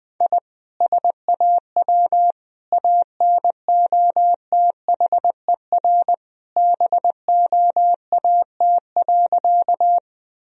Morse code, heard as a keyed tone, about 20 wpm, 700 Hz. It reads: I SAW ANOTHER BOAT.
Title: Sentences from Top 1000 Words - Review Part 3 - 20wpm